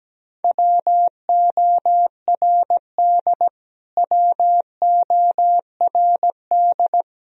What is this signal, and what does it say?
Morse code, heard as a keyed tone, about 17 wpm, 700 Hz. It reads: WORD WORD
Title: Top 100 Words Repeated 17wpm